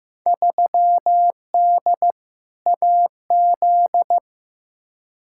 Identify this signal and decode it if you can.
Morse code, heard as a keyed tone, about 15 wpm, 700 Hz. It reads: 3D AZ
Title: Field Day Exchanges - 15wpm